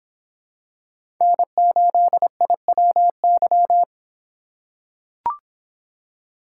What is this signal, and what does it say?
Morse code, heard as a keyed tone, about 25 wpm, 700 Hz. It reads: N8IWY